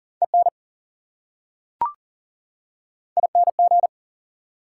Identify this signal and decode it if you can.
Morse code, heard as a keyed tone, about 40 wpm, 700 Hz. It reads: EN ING
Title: Prefixes and Suffixes 40wpm